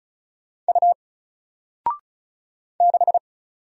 Morse code, heard as a keyed tone, about 33 wpm, 700 Hz. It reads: U 6
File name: Single-Letter-Number-35wpm